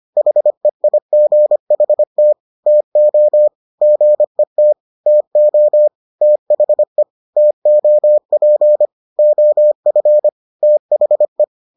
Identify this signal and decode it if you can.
Morse code, heard as a keyed tone, about 25 wpm, 600 Hz. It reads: HEIGHT TO GET TO THE TOP OF THE